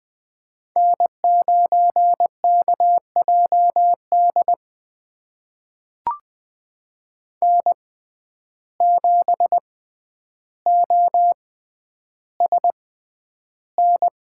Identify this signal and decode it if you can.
Morse code, heard as a keyed tone, about 21 wpm, spaced as for 10 wpm, 700 Hz. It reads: N9KJD N7OSN
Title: USA Call Signs - 5 Characters - 6x Character Spacing - 20wpm